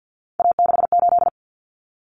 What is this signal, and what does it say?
Morse code, heard as a keyed tone, about 51 wpm, 700 Hz. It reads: A68
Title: Common Cut Numbers - 3-Digit 50wpm